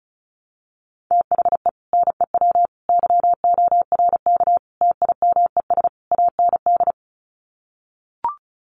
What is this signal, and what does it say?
Morse code, heard as a keyed tone, about 34 wpm, 700 Hz. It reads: THE NEW YORK TIMES AND
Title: Sets of 5 Words Encore 35wpm